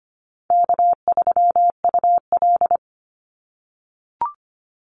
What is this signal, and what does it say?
Morse code, heard as a keyed tone, about 25 wpm, 700 Hz. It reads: K3UL
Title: Call Signs - 25wpm